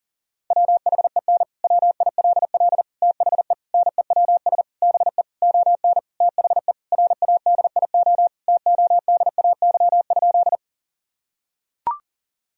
Morse code, heard as a keyed tone, about 40 wpm, 700 Hz. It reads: WHEN WILL THE NEWS BE ON THE RADIO TODAY?
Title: Sentences from Top 1000 Words - Review Part 3 - 40wpm